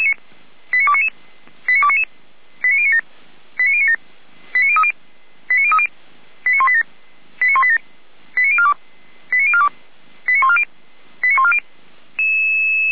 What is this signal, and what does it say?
Five-tone selective call (ZVEI), heard as quick runs of five tones, tones of about 70 ms. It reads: 89299 89299 89097 89097 89030 89030 89187 89187 89042 89042 89140 89140